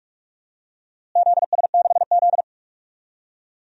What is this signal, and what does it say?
Morse code, heard as a keyed tone, about 45 wpm, 700 Hz. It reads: ZS6Z
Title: DXCC-100 Call Signs - 4 Characters - Repeated 2x - 45wpm